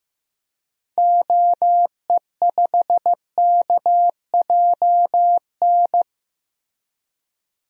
Morse code, heard as a keyed tone, about 16 wpm, 700 Hz. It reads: OE5KJN
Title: DXCC-100 Call Signs - 6 Characters - Repeated 3x - 15wpm